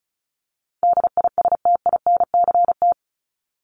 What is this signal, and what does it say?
Morse code, heard as a keyed tone, about 34 wpm, 700 Hz. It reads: DISTINCT
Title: Top 1501-1750 Words 35wpm